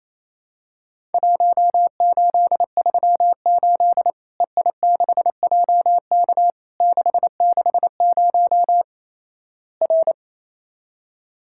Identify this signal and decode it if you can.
Morse code, heard as a keyed tone, about 28 wpm, 700 Hz. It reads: 1838 EI6JK 660 R
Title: Worked All Europe DX Contest - QTC Exchange - 28wpm